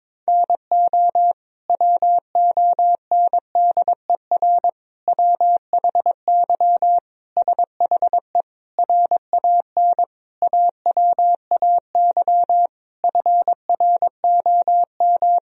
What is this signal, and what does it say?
Morse code, heard as a keyed tone, about 21 wpm, 700 Hz. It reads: NO WONDER WHY SHE RAN AWAY FROM
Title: Sentences from Top 1000 Words - Review Part 3 - 22wpm